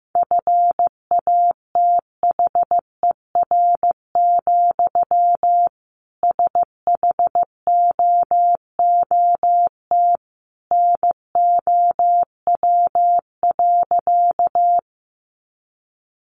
Morse code, heard as a keyed tone, about 15 wpm, 700 Hz. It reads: FATHER, SHOOT NOW.
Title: Sentences from Top 1200 Words 15wpm